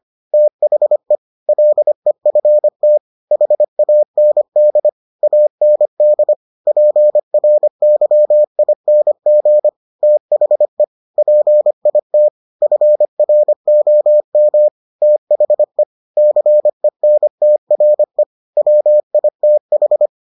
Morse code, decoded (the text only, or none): THE LEFT HAND AND PRYING THE PIT FROM THE CENTRE WITH